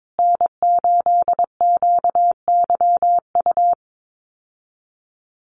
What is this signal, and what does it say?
Morse code, heard as a keyed tone, about 22 wpm, 700 Hz. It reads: N8QYU
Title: USA Call Signs Repeated 2x 22wpm